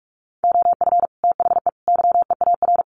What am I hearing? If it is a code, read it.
Morse code, heard as a keyed tone, about 44 wpm, 700 Hz. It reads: OF THE YEAR
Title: Sets of 3 Words 45wpm